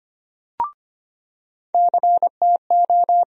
Morse code, heard as a keyed tone, about 25 wpm, 700 Hz. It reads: CTO
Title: USA Call Sign Postfixes - 3 Letters 25wpm